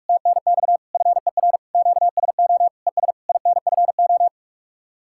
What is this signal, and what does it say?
Morse code, heard as a keyed tone, about 44 wpm, 700 Hz. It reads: TNX FER QSO ES INFO